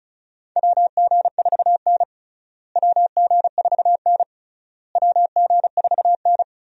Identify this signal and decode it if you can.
Morse code, heard as a keyed tone, about 36 wpm, 700 Hz. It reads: WG4N WG4N WG4N
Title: USA Call Signs - 4 Characters - Repeated 3x - 35wpm